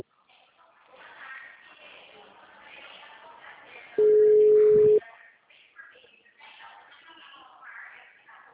Ringback tone (Europe, 425 Hz).
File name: out- -2004- - - .151